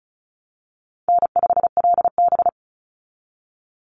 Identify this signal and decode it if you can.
Morse code, heard as a keyed tone, about 34 wpm, 700 Hz. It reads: N5LB
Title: Call Signs Repeated 3x - 35wpm